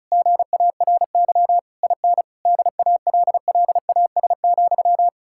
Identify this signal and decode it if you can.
Morse code, heard as a keyed tone, about 48 wpm, spaced as for 42 wpm, 700 Hz. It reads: GARY IN DALLAS,
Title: Speed-Racing Common QSO Phrases - 35wpm